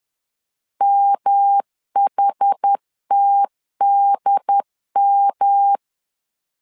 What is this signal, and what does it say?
Morse code, heard as a keyed tone, about 10 wpm, 800 Hz. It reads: MHTDM
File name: LEZIONI / 4 Q1 10wpm 2448